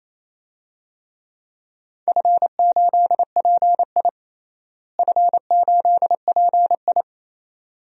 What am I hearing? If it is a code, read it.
Morse code, heard as a keyed tone, about 27 wpm, 700 Hz. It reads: F8PI F8PI
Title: DXCC-100 Call Signs - 4 Characters - Repeated 2x - 28wpm